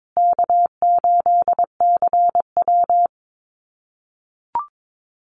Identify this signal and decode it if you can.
Morse code, heard as a keyed tone, about 22 wpm, 700 Hz. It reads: K8CW
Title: USA Call Signs Repeated 3x 22wpm